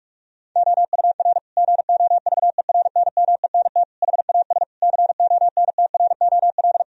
Morse code, heard as a keyed tone, about 46 wpm, 700 Hz. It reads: OUR GOVERNMENT HAS CONTROL
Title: Sentences from Top 1000 Words - Review Part 4 - 45wpm